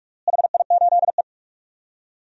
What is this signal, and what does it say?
Morse code, heard as a keyed tone, about 46 wpm, 700 Hz. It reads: HI8E
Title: DXCC-100 Call Signs - 4 Characters - Repeated 3x - 45wpm